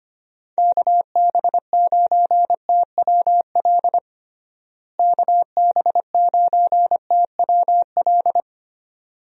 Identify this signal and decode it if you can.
Morse code, heard as a keyed tone, about 25 wpm, 700 Hz. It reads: KB9TWL KB9TWL
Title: USA Call Signs - 6 Characters - Repeated 2x - 25wpm